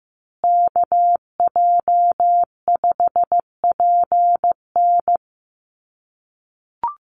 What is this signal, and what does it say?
Morse code, heard as a keyed tone, about 15 wpm, 700 Hz. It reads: KJ5PN
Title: USA Call Signs Repeated 2x 15wpm